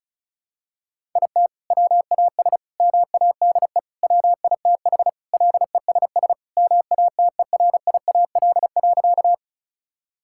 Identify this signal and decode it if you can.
Morse code, heard as a keyed tone, about 34 wpm, 700 Hz. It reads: IT WAS MADE WITH LESS MATERIAL.
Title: Sentences from Top 1000 Words - Review Part 5 - 35wpm